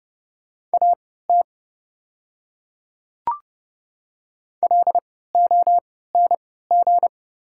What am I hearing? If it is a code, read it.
Morse code, heard as a keyed tone, about 30 wpm, spaced as for 17 wpm, 690 Hz. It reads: AT LONG
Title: 3x Character Spacing - Top 200 Words - 30wpm